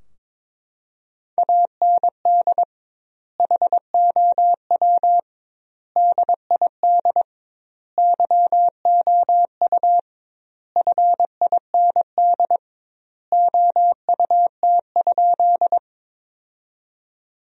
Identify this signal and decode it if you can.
Morse code, heard as a keyed tone, about 22 wpm, 700 Hz. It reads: AND HOW DID YOU FIND OUT?
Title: 2x Word-Spacing - Sentences from Top 100 Words - 22wpm